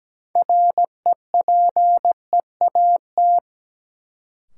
Morse code, heard as a keyed tone, about 17 wpm, 700 Hz. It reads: REPEAT
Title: Sentences from Top 1600 Words - New Word Drill Alt - 17wpm